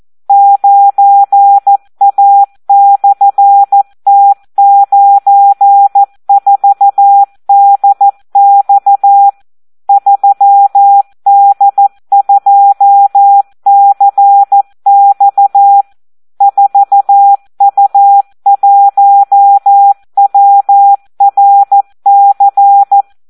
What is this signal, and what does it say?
Morse code, heard as a keyed tone, about 14 wpm, 800 Hz. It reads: 9A/T94DX 3D2CX 4U1WRC
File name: Audio practica CW / SD 1 14